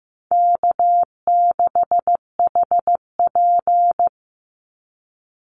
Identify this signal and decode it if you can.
Morse code, heard as a keyed tone, about 15 wpm, 700 Hz. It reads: K6HP
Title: Easy Call Signs - 15wpm